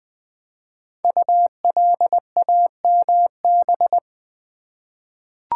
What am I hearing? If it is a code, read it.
Morse code, heard as a keyed tone, about 20 wpm, 700 Hz. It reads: ULAMB
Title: Code Groups - 5 Letters - 20wpm